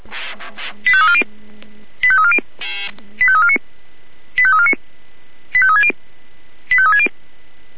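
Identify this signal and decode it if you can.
Five-tone selective call (ZVEI), heard as quick runs of five tones, tones of about 70 ms. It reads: 85340 85340 85349 85349 85360 85360
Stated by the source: ZVEI five-tone call received by SDR.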